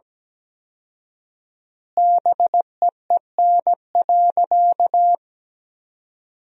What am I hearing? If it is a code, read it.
Morse code, heard as a keyed tone, about 17 wpm, 700 Hz. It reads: BEEN.
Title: 4x Word-Spacing - Sentences from Top 300 Words - 17wpm